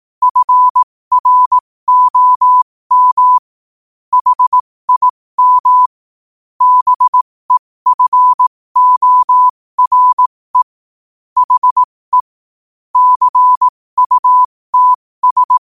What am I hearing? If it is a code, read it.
Morse code, heard as a keyed tone, about 18 wpm, spaced as for 15 wpm, 1000 Hz. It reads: FROM HIM BEFORE HE CUTS
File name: qotdfiles / 15wpm